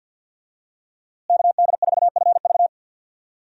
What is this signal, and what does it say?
Morse code, heard as a keyed tone, about 48 wpm, 700 Hz. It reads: KD4FV
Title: USA Call Signs - 5 Characters - Repeated 2x - 50wpm